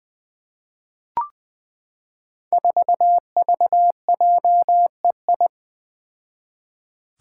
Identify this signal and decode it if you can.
Morse code, heard as a keyed tone, about 20 wpm, 700 Hz. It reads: 4VJEI
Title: Five Challenging Letter-Number Combinations 20wpm